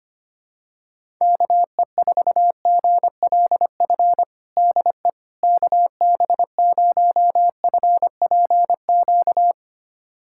Morse code, heard as a keyed tone, about 25 wpm, 700 Hz. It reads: KE4GLF DE KB0FPQ
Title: USA Callsign DE Callsign 25wpm